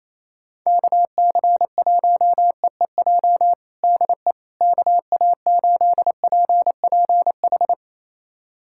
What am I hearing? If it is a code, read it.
Morse code, heard as a keyed tone, about 28 wpm, 700 Hz. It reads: KC1EEJ DE KA8PPH
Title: USA Callsign DE Callsign - 6 Characters - 28wpm